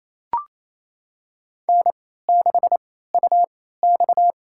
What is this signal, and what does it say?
Morse code, heard as a keyed tone, about 28 wpm, spaced as for 16 wpm, 700 Hz. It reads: N6UX
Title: USA Call Signs - 4 Characters - 3x Character Spacing - 28wpm